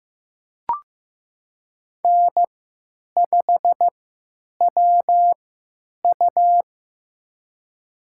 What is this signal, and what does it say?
Morse code, heard as a keyed tone, about 16 wpm, spaced as for 9 wpm, 700 Hz. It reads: N5WU
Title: USA Call Signs - 4 Characters - 3x Character Spacing - 15wpm